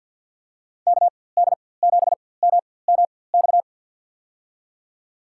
Morse code, heard as a keyed tone, about 49 wpm, spaced as for 23 wpm, 700 Hz. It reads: KD7MMX
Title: USA Call Signs - 6 Characters - 4x Character Spacing - 50wpm